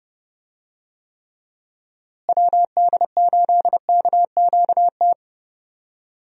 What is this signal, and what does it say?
Morse code, heard as a keyed tone, about 30 wpm, 700 Hz. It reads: WD8KQT
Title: USA Call Signs - 6 Characters - Repeated 3x - 30wpm